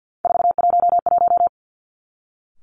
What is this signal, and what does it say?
Morse code, heard as a keyed tone, about 50 wpm, 700 Hz. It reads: 411
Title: Three Challenging Letter-Number Combinations 50wpm